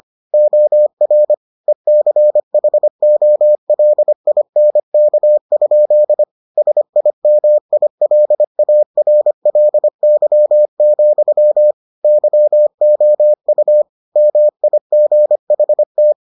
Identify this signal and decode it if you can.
Morse code, heard as a keyed tone, about 25 wpm, 600 Hz. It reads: OR ECHOLINK? SIMILARLY, YOU MIGHT